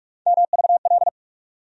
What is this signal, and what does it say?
Morse code, heard as a keyed tone, about 44 wpm, 700 Hz. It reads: MVL
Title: USA Call Sign Postfixes - 3 Letters 45wpm